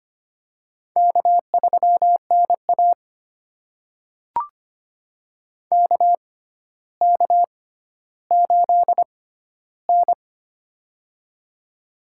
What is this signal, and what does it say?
Morse code, heard as a keyed tone, about 25 wpm, spaced as for 13 wpm, 700 Hz. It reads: K3NA KK8N
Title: USA Call Signs - 4 Characters - 6x Character Spacing - 25wpm